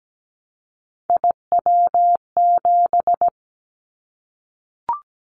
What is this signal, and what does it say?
Morse code, heard as a keyed tone, about 17 wpm, 700 Hz. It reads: IW7
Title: Callsign Prefixes for Top 25 DXCC - 17wpm